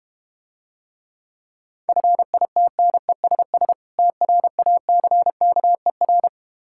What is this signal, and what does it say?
Morse code, heard as a keyed tone, about 32 wpm, 700 Hz. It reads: FITNESS TRACKER